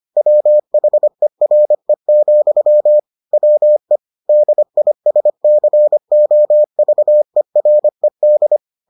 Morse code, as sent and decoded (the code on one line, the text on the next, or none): .-- .... .-. --..-- / .-- . / -.. .. ... -.-. --- ...- . .-. -..
WHERE, WE DISCOVERED